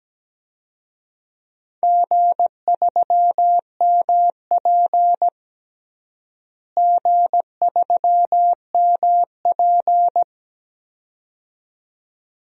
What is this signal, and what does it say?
Morse code, heard as a keyed tone, about 17 wpm, 700 Hz. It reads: G3MP G3MP
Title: DXCC-100 Call Signs - 4 Characters - Repeated 2x - 17wpm